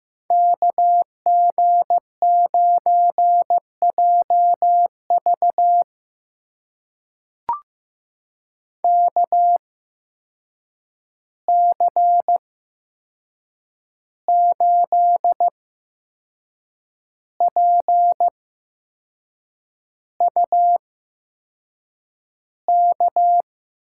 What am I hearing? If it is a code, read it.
Morse code, heard as a keyed tone, about 15 wpm, spaced as for 6 wpm, 700 Hz. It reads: KG9JV KC8PUK
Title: USA Call Signs - 8x Character Spacing - 15wpm